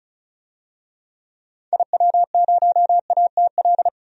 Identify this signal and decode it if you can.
Morse code, heard as a keyed tone, about 36 wpm, 700 Hz. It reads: IW0ATL
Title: DXCC-100 Call Signs - 6 Characters - 35wpm